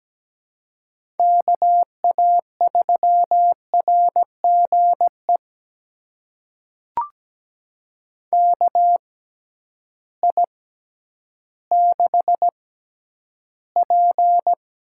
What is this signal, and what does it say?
Morse code, heard as a keyed tone, about 17 wpm, spaced as for 10 wpm, 700 Hz. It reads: KA3RGE KI6P
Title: USA Call Signs - 6 Characters - 6x Character Spacing - 17wpm